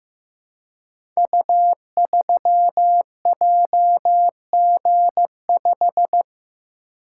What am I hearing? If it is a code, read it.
Morse code, heard as a keyed tone, about 15 wpm, 700 Hz. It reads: U3JG5
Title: Code Groups - 5 Letter:Numbers - 15wpm